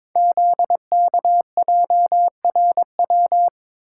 Morse code, heard as a keyed tone, about 22 wpm, 700 Hz. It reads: ZKJRW
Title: Code Groups - 5 Letter:Numbers - 22wpm